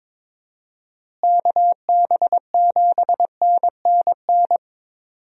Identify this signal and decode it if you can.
Morse code, heard as a keyed tone, about 22 wpm, 700 Hz. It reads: KB7NNN
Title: USA Call Signs - 6 Characters - Repeated 2x - 22wpm